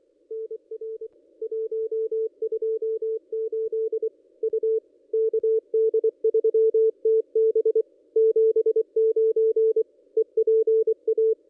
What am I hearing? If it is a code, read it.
Morse code, heard as a keyed tone, about 24 wpm, 440 Hz. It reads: NR 128 U KD3TB 79 EPA
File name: contests / ss-cw-k5tr-op-n5rz -1600